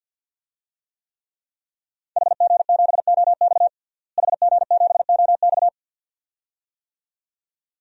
Morse code, heard as a keyed tone, about 50 wpm, 700 Hz. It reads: HG7OX HG7OX
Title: DXCC-100 Call Signs - 5 Characters - Repeated 2x - 50wpm